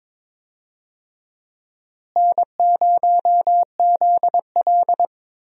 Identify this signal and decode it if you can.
Morse code, heard as a keyed tone, about 22 wpm, 700 Hz. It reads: N0ZL
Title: USA Call Signs - 4 Characters - Repeated 2x - 22wpm